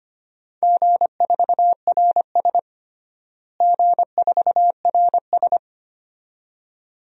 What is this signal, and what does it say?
Morse code, heard as a keyed tone, about 25 wpm, 700 Hz. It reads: G4RS G4RS
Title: DXCC-100 Call Signs - 4 Characters - Repeated 2x - 25wpm